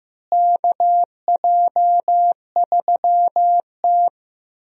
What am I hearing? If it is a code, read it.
Morse code, heard as a keyed tone, about 15 wpm, 700 Hz. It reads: KJ3T
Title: USA Call Signs - 4 Characters - Repeated 3x - 15wpm